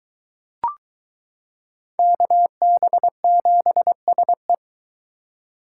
Morse code, heard as a keyed tone, about 22 wpm, 700 Hz. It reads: KB7SE